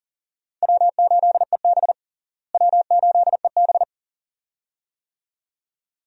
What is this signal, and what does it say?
Morse code, heard as a keyed tone, about 40 wpm, 700 Hz. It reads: W8EB W8EB
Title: USA Call Signs - 4 Characters - Repeated 2x - 40wpm